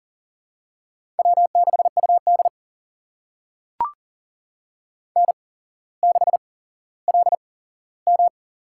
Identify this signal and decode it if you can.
Morse code, heard as a keyed tone, about 40 wpm, spaced as for 17 wpm, 700 Hz. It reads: W6UD N6LM
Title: USA Call Signs - 4 Characters - 8x Character Spacing - 40wpm